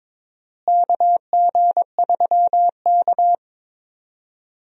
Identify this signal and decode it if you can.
Morse code, heard as a keyed tone, about 22 wpm, 700 Hz. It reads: KG3K
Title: USA Call Signs - 4 Characters - Repeated 2x - 22wpm